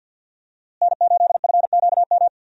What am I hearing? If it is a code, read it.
Morse code, heard as a keyed tone, about 53 wpm, 700 Hz. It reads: N8FQM
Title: USA Call Signs - 5 Characters - Repeated 2x - 50wpm